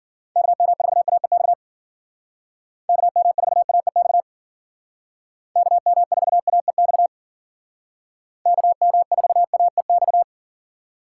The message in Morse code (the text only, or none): KM4AEX KM4AEX KM4AEX KM4AEX